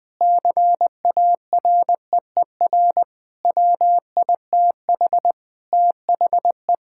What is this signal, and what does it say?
Morse code, heard as a keyed tone, about 20 wpm, 700 Hz. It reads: CAREER WITH THE
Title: Sentences from Top 1000 Words - Review Part 6 - 20wpm